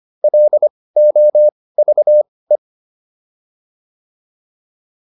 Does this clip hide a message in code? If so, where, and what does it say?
Yes: Morse code from 0.2 to 2.6 s, heard as a keyed tone, about 25 wpm, spaced as for 18 wpm, 600 Hz. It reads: LOVE